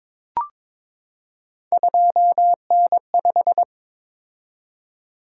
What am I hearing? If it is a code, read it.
Morse code, heard as a keyed tone, about 22 wpm, 700 Hz. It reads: 2N5
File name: practice / 22wpm / Three Letter-Number Combinations - 22wpm